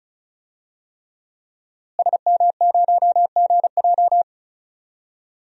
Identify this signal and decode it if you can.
Morse code, heard as a keyed tone, about 34 wpm, 700 Hz. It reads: SM0GJ
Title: DXCC-100 Call Signs - 5 Characters - Repeated 3x - 35wpm